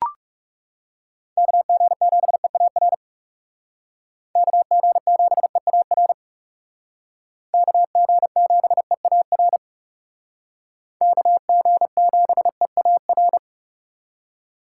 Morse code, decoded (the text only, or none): KG7EAR KG7EAR KG7EAR KG7EAR